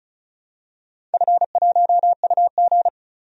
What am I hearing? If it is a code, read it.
Morse code, heard as a keyed tone, about 34 wpm, 700 Hz. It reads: F1UG
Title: DXCC-100 Call Signs - 4 Characters - 35wpm